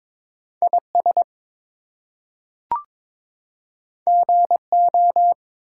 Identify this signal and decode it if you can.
Morse code, heard as a keyed tone, about 22 wpm, 700 Hz. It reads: IS GO
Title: Two Letter Words - 22wpm